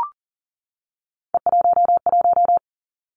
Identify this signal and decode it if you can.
Morse code, heard as a keyed tone, about 39 wpm, 700 Hz. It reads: E11
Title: Three Challenging Letter-Number Combinations 40wpm